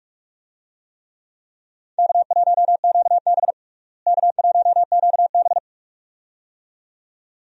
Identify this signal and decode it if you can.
Morse code, heard as a keyed tone, about 46 wpm, 700 Hz. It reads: K1QB K1QB
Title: USA Call Signs - 4 Characters - Repeated 2x - 45wpm